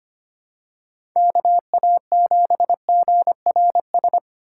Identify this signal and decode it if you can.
Morse code, heard as a keyed tone, about 25 wpm, 700 Hz. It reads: KA7GRS
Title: USA Call Signs - 6 Characters - 6x Character Spacing - 25wpm